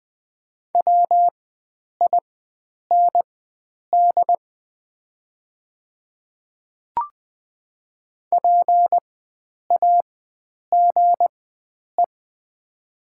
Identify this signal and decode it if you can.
Morse code, heard as a keyed tone, about 20 wpm, spaced as for 9 wpm, 700 Hz. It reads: WIND PAGE
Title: 4x Character Spacing - Top 500 Words - 20wpm